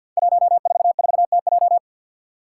Morse code, heard as a keyed tone, about 53 wpm, 700 Hz. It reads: 144TJ
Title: Five Challenging Letter-Number Combinations 50wpm